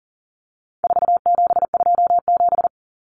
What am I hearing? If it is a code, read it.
Morse code, heard as a keyed tone, about 40 wpm, 700 Hz. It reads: 4727
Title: Numbers - 4-Digit 40wpm